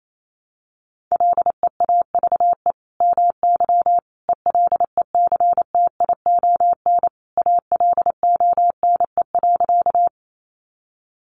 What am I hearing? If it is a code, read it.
Morse code, heard as a keyed tone, about 28 wpm, 700 Hz. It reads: LEAVE MY ELECTION ALONE.
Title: Sentences from Top 1500 Words 28wpm